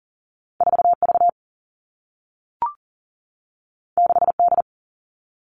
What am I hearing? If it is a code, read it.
Morse code, heard as a keyed tone, about 40 wpm, 700 Hz. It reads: 4 V 6 D
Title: Two Challenging Letter-Number Combinations 40wpm